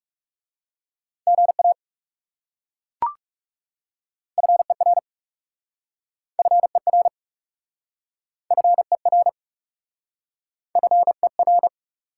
Morse code, heard as a keyed tone, about 37 wpm, 700 Hz. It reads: GA FER FER FER FER
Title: Speed-Racing Top 180 QSO Elements 30wpm